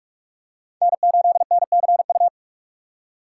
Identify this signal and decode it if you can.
Morse code, heard as a keyed tone, about 41 wpm, 700 Hz. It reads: N8NCU
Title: USA Call Signs - 5 Characters - 3x Character Spacing - 45wpm